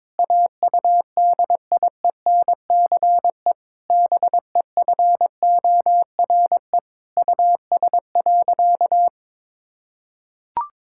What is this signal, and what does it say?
Morse code, heard as a keyed tone, about 22 wpm, 700 Hz. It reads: AUDIENCE BEFORE US.
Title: Sentences from Top 1000 Words - Review Part 5 - 22wpm